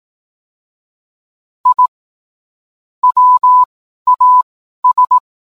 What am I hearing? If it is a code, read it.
Morse code, heard as a keyed tone, about 18 wpm, spaced as for 12 wpm, 1000 Hz. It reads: I WAS